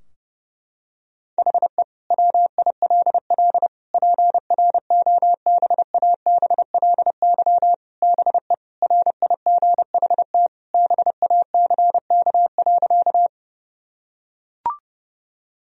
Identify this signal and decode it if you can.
Morse code, heard as a keyed tone, about 30 wpm, 700 Hz. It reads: HE WILL PROBABLY BE RIGHT BACK.
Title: Sentences from Top 1000 Words - Review Part 4 - 30wpm